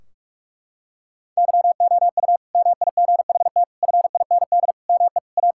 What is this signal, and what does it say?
Morse code, heard as a keyed tone, about 43 wpm, 700 Hz. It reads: YOU MIGHT FIND ME A
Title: Sentences from Top 1600 Words 45wpm